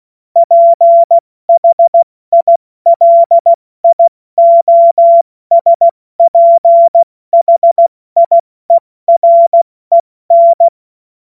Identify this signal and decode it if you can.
Morse code, heard as a keyed tone, about 16 wpm, 680 Hz. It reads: PHILIOSPHIEREN
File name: Morsekurs / Witze / Witz 21.November 2023 70 F80